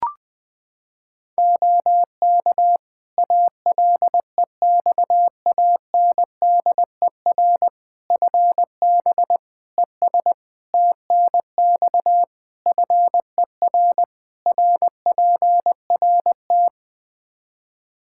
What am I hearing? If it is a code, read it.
Morse code, heard as a keyed tone, about 20 wpm, 700 Hz. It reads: OK ALEXANDER FB ES TNX FER RPRT
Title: K1USN SST QSO Practice 20wpm